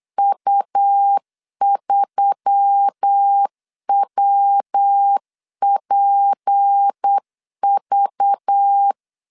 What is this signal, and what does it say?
Morse code, heard as a keyed tone, about 8 wpm, 800 Hz. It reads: U3WPV
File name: CW / DET / 1 6 1500C S C 8wpm 5219